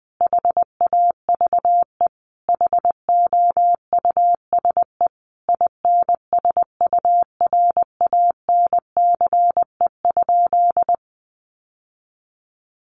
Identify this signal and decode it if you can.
Morse code, heard as a keyed tone, about 20 wpm, 700 Hz. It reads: HAVE HOUSE INSURANCE?
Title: Sentences from Top 1100 Words 20wpm